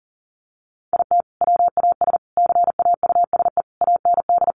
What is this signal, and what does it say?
Morse code, heard as a keyed tone, about 40 wpm, 700 Hz. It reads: IT WAS CAUSE AND
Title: Sentences from Top 1000 Words 40wpm